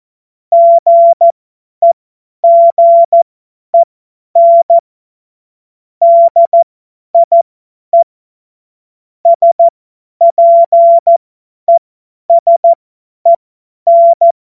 Morse code, heard as a keyed tone, about 14 wpm, spaced as for 10 wpm, 680 Hz. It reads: GEGEN DIE SPESEN
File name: Morsekurs / Witze / Witz 23. August 2022 50 F70